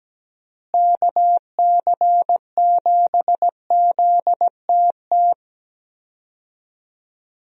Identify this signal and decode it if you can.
Morse code, heard as a keyed tone, about 17 wpm, 700 Hz. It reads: KC7ZTT
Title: USA Call Signs - 6 Characters - Repeated 3x - 17wpm